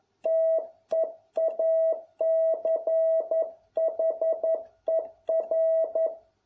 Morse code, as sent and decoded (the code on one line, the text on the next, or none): - . .- -.-. .... .-.
TEACHER